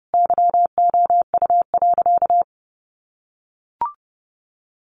Morse code, heard as a keyed tone, about 30 wpm, 700 Hz. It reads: YOU.
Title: Sentences from Top 500 Words 30wpm